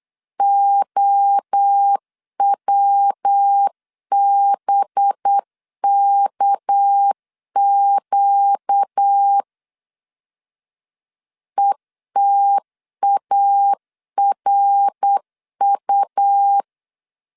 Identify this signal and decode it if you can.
Morse code, heard as a keyed tone, about 8 wpm, 800 Hz. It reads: OWBKQ ETARU
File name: LEZIONI / 6 PKX0 / 6 PK 8wpm 3003